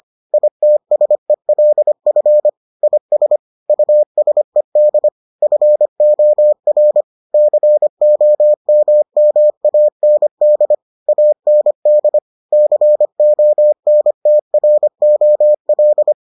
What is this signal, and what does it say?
Morse code, heard as a keyed tone, about 25 wpm, 600 Hz. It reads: ITSELF IS USED FOR COMMAND AND CONTROL